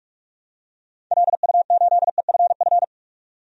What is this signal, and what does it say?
Morse code, heard as a keyed tone, about 44 wpm, 700 Hz. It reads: LU8EFF
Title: DXCC-100 Call Signs - 6 Characters - 45wpm